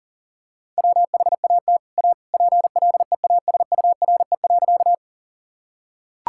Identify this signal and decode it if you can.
Morse code, heard as a keyed tone, about 40 wpm, 700 Hz. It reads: WHAT A PLEASURE.
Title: Sentences from Top 1600 Words 40wpm